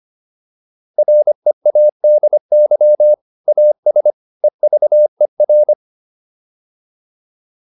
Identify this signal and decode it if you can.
Morse code, heard as a keyed tone, about 25 wpm, 600 Hz. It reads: READY AS EVER